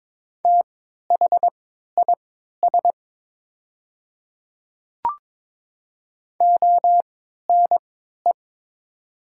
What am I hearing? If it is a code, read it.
Morse code, heard as a keyed tone, about 22 wpm, spaced as for 13 wpm, 690 Hz. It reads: THIS ONE
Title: 3x Character Spacing - Top 100 Words - 22wpm